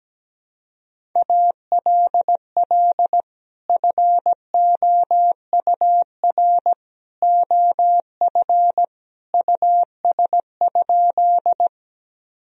Morse code, heard as a keyed tone, about 17 wpm, 700 Hz. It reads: ALL FOUR OF US?
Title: Sentences from Top 1000 Words - Review Part 1 - 17wpm